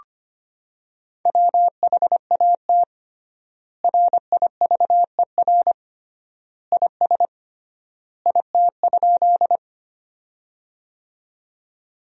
Morse code, heard as a keyed tone, about 25 wpm, 700 Hz. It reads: WHAT RIVER IS IT?
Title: 3x Word-Spacing - Sentences from Top 300 Words - 25wpm